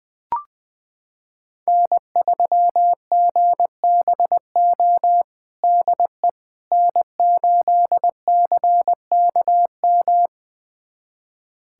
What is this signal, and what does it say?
Morse code, heard as a keyed tone, about 20 wpm, 700 Hz. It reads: N3GBO DE N8CKM
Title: USA Callsign DE Callsign - 5 Characters - 20wpm